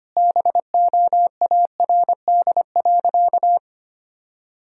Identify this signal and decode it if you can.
Morse code, heard as a keyed tone, about 25 wpm, 700 Hz. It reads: BOARD.
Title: Sentences from Top 1000 Words - Review Part 4 - 25wpm